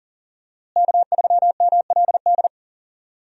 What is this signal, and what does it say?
Morse code, heard as a keyed tone, about 40 wpm, 700 Hz. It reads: K3MLD
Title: USA Call Signs - 5 Characters - Repeated 2x - 40wpm